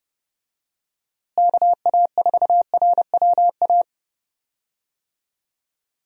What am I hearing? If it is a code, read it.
Morse code, heard as a keyed tone, about 32 wpm, 700 Hz. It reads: KA4RWA
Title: USA Call Signs - 6 Characters - Repeated 2x - 30wpm